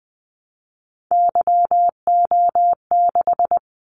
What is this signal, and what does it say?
Morse code, heard as a keyed tone, about 20 wpm, 700 Hz. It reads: YO6
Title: Callsign Prefixes for Top 25 DXCC - 20wpm